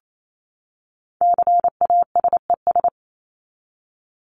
Morse code, heard as a keyed tone, about 28 wpm, 700 Hz. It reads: CASES
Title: ES:S Suffix - Top 100 Words with Base Words 28wpm